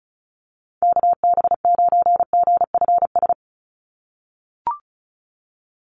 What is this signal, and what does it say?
Morse code, heard as a keyed tone, about 34 wpm, 700 Hz. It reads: KB9GFS
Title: USA Call Signs 35wpm